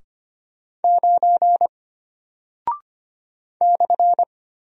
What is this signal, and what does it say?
Morse code, heard as a keyed tone, about 25 wpm, 700 Hz. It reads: 9 /
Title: Single Character (Letter:Number:Punctuation:Prosign) Alt 25wpm